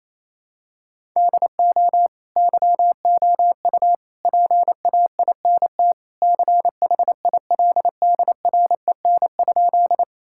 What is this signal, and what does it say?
Morse code, heard as a keyed tone, about 28 wpm, 700 Hz. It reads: DO YOU PAINT CHILDREN?
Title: Sentences from Top 1000 Words - Review Part 6 - 28wpm